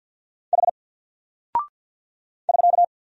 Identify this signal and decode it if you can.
Morse code, heard as a keyed tone, about 48 wpm, 700 Hz. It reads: H <SK>
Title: Single Character (Letter:Number:Punctuation:Prosign) Alt 50wpm